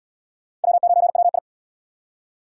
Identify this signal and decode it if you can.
Morse code, heard as a keyed tone, about 76 wpm, 700 Hz. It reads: W9WI